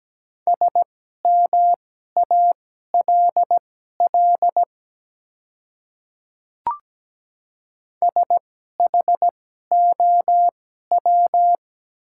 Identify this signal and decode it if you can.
Morse code, heard as a keyed tone, about 17 wpm, spaced as for 12 wpm, 700 Hz. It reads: SMALL SHOW
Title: 2x Character Spacing - Top 200 Words - 17wpm